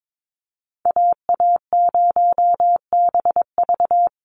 Morse code, heard as a keyed tone, about 22 wpm, 700 Hz. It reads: AA0BV
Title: NA SOTA Activator Callsigns 22wpm